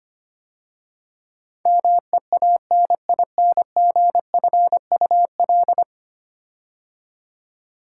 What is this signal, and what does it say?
Morse code, heard as a keyed tone, about 25 wpm, 700 Hz. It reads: MEANINGFUL
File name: 10 Letter Words - First 3 Letters Given - 25wpm